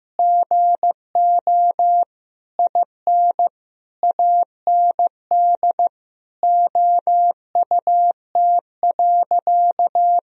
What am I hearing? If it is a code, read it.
Morse code, heard as a keyed tone, about 15 wpm, 700 Hz. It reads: GO IN AND OUT.
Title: Sentences from Top 1000 Words - Review Part 2 - 15wpm